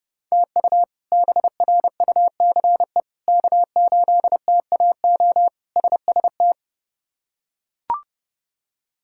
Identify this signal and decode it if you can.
Morse code, heard as a keyed tone, about 30 wpm, 700 Hz. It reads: TU BRUCE K8TAO SST
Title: K1USN SST QSO Practice - 30wpm